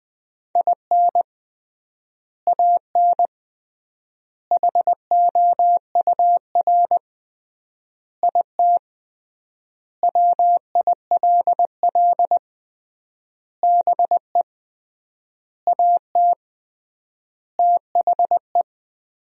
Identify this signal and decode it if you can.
Morse code, heard as a keyed tone, about 20 wpm, 700 Hz. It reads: IN AN HOUR IT WILL BE AT THE
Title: 3x Word-Spacing - Sentences from Top 400 Words - 20wpm